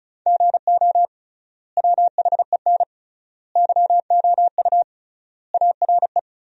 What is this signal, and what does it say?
Morse code, heard as a keyed tone, about 34 wpm, 700 Hz. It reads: GO WHEN YOU ARE
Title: 3x Word-Spacing - Sentences from Top 500 Words - 35wpm